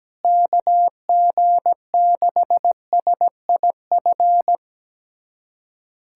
Morse code, heard as a keyed tone, about 17 wpm, 700 Hz. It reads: KG6SIF
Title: USA Callsign DE Callsign 17wpm